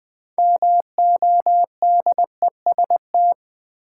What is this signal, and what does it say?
Morse code, heard as a keyed tone, about 20 wpm, 700 Hz. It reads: MODEST
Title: 6 Letter Words - First 3 Letters Given - 20wpm